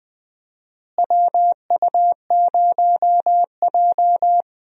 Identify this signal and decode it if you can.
Morse code, heard as a keyed tone, about 21 wpm, 700 Hz. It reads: WU0J
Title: USA Call Signs - 4 Characters - Repeated 3x - 20wpm